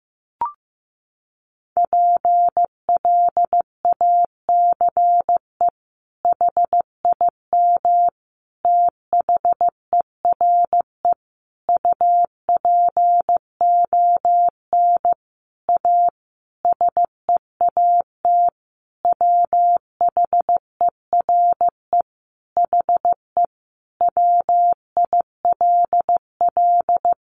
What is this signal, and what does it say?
Morse code, heard as a keyed tone, about 15 wpm, 700 Hz. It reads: PLACE HIM THERE UPON A SEAT WHERE HE WILL
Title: Sentences from Top 1500 Words 15wpm